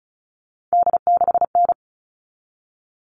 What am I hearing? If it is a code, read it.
Morse code, heard as a keyed tone, about 34 wpm, 700 Hz. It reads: D6N
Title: All Cut Numbers - 3-Digit 35wpm